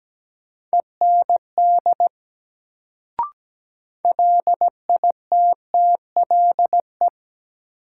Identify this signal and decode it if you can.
Morse code, heard as a keyed tone, about 17 wpm, 700 Hz. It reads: END LITTLE
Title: Sentences from Top 200 Words - New Word Drill - 17wpm